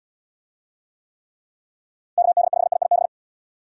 Morse code, heard as a keyed tone, about 69 wpm, 690 Hz. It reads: KD5IIB